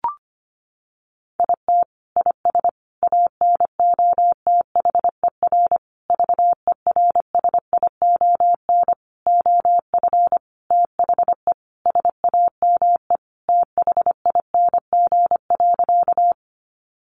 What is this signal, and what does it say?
Morse code, heard as a keyed tone, about 25 wpm, 700 Hz. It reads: IT IS ANOTHER VERSION OF THE SAME THING.
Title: Sentences from Top 1400 Words 25wpm